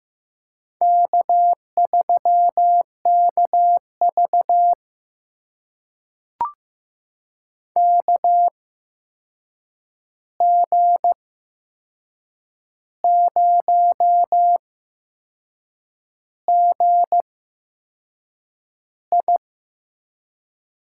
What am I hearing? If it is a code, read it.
Morse code, heard as a keyed tone, about 15 wpm, spaced as for 6 wpm, 700 Hz. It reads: K3KV KG0GI
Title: USA Call Signs - 8x Character Spacing - 15wpm